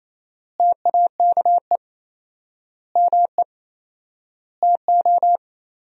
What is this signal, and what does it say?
Morse code, heard as a keyed tone, about 28 wpm, 700 Hz. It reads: TAKE ME TO
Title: 4x Word-Spacing - Sentences from Top 200 Words - 28wpm